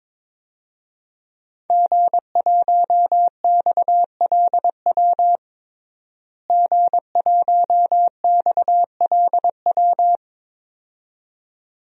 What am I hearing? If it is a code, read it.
Morse code, heard as a keyed tone, about 22 wpm, 700 Hz. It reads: G1XLW G1XLW
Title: DXCC-100 Call Signs - 5 Characters - Repeated 2x - 22wpm